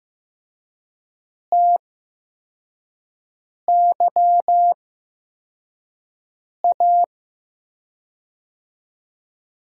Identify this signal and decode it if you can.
Morse code, heard as a keyed tone, about 15 wpm, spaced as for 4 wpm, 700 Hz. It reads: TYA
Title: USA Call Signs - 6 Characters - 8x Character Spacing - 15wpm